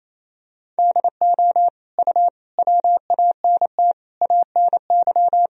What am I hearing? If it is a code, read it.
Morse code, heard as a keyed tone, about 28 wpm, 700 Hz. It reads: DO U WANT ANY
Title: Sentences from Top 400 Words and Abbreviations 28wpm